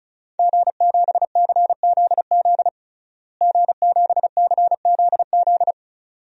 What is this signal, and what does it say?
Morse code, heard as a keyed tone, about 36 wpm, 700 Hz. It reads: G7CZZ G7CZZ
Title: DXCC-100 Call Signs - 5 Characters - Repeated 2x - 35wpm